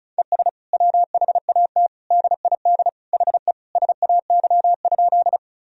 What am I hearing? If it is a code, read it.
Morse code, heard as a keyed tone, about 36 wpm, 700 Hz. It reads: ES WHAT DID HE SAY?
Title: Sentences from Top 200 Words and Abbreviations 35wpm